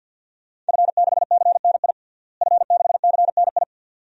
Morse code, heard as a keyed tone, about 52 wpm, 700 Hz. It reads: F6CNI F6CNI
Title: DXCC-100 Call Signs - 5 Characters - Repeated 2x - 50wpm